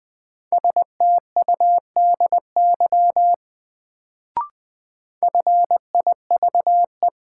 Morse code, heard as a keyed tone, about 20 wpm, 700 Hz. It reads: STUDY FIVE
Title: Sets of 5 Words - New Word Drill - 20wpm